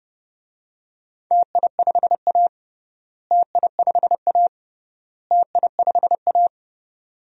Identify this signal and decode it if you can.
Morse code, heard as a keyed tone, about 29 wpm, 700 Hz. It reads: TI5A TI5A TI5A
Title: DXCC-100 Call Signs - 4 Characters - Repeated 3x - 30wpm